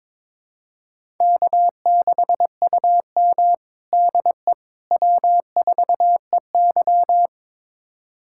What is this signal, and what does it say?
Morse code, heard as a keyed tone, about 21 wpm, 700 Hz. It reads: K6UM DE W4EY
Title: USA Callsign DE Callsign - 4 Characters - 22wpm